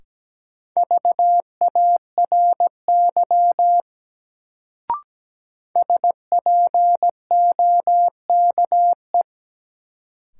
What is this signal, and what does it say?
Morse code, heard as a keyed tone, about 17 wpm, 700 Hz. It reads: VARY SPOKE
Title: Sentences from Top 1200 Words - New Word Drill - 17wpm